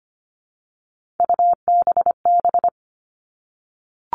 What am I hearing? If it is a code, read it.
Morse code, heard as a keyed tone, about 25 wpm, 700 Hz. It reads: UBB
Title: All Cut Numbers - 3-Digit 25wpm